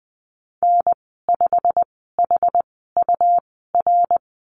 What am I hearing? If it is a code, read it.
Morse code, heard as a keyed tone, about 19 wpm, spaced as for 14 wpm, 700 Hz. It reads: N5HUR
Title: 154 Lesson 25g - Letter p - Call Signs 20wpm